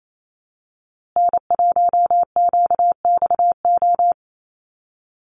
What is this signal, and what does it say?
Morse code, heard as a keyed tone, about 28 wpm, 700 Hz. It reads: N1QXO
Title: USA Call Signs Repeated 2x 28wpm